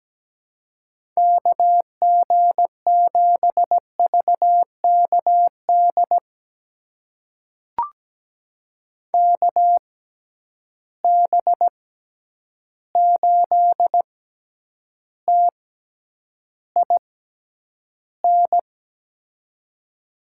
Morse code, heard as a keyed tone, about 17 wpm, spaced as for 9 wpm, 700 Hz. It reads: KG7VKD KB8TIN
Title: USA Call Signs - 6 Characters - 6x Character Spacing - 17wpm